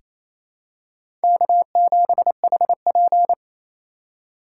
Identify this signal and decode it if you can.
Morse code, heard as a keyed tone, about 28 wpm, 700 Hz. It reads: K7HP
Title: USA Call Signs - 4 Characters - 3x Character Spacing - 28wpm